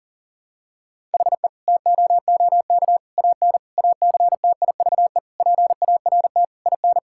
Morse code, heard as a keyed tone, about 40 wpm, 700 Hz. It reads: HE TOOK AN ACTIVE PART IN
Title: Sentences from Top 1000 Words - Review Part 6 - 40wpm